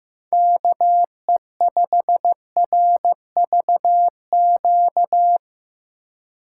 Morse code, heard as a keyed tone, about 15 wpm, 700 Hz. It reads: KE5RVQ
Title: USA Call Signs - 6 Characters - 4x Character Spacing - 15wpm